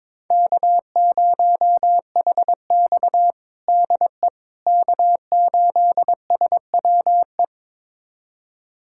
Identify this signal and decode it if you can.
Morse code, heard as a keyed tone, about 22 wpm, 700 Hz. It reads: K0HX DE K8SWE
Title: USA Callsign DE Callsign 22wpm